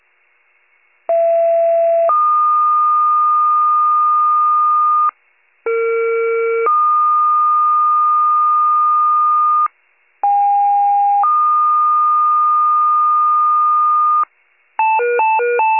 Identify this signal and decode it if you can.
Two-tone page, heard as a one-second tone, then a three-second tone, about 670 then 1220 Hz, about 460 then 1220 Hz, about 810 then 1220 Hz. It is heard from 1.1 to 14.2 s.
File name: pager / 21 RESCUE / wcalert